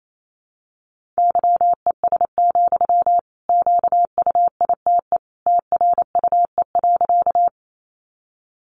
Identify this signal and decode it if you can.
Morse code, heard as a keyed tone, about 28 wpm, 700 Hz. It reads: YES, QUITE TRUE.
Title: Sentences from Top 700 Words 28wpm